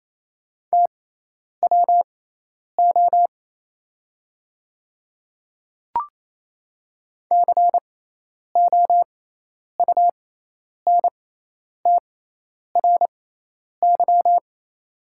Morse code, heard as a keyed tone, about 28 wpm, spaced as for 10 wpm, 700 Hz. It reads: TWO COUNTRY
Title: 6x Character Spacing - Top 300 Words - 28wpm